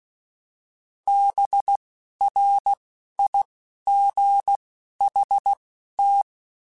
Morse code, heard as a keyed tone, about 16 wpm, spaced as for 11 wpm, 800 Hz. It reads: BRIGHT
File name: Spaced Characters / Spaced Characters-mp3s / Weather 5 5 sp @15